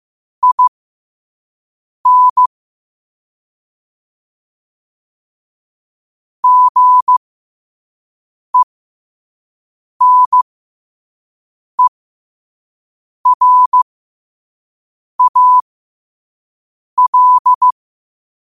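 Morse code, heard as a keyed tone, about 15 wpm, spaced as for 5 wpm, 1000 Hz. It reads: IN GENERAL
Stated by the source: Morse at 5 wpm.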